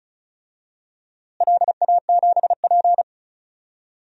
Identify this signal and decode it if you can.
Morse code, heard as a keyed tone, about 34 wpm, 700 Hz. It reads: LA7P
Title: DXCC-100 Call Signs - 4 Characters - Repeated 2x - 35wpm